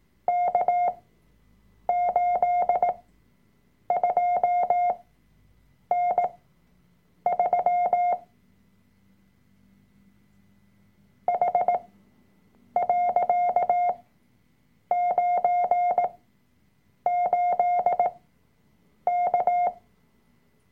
Morse code, heard as a keyed tone, about 18 wpm, spaced as for 7 wpm, 710 Hz. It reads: K82N3 H.98K
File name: Workshop Dateien / CW-Online-Kurs / Lektion 003